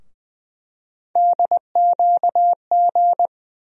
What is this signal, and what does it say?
Morse code, heard as a keyed tone, about 20 wpm, 700 Hz. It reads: DQG
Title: USA Call Sign Postfixes - 3 Letters 20wpm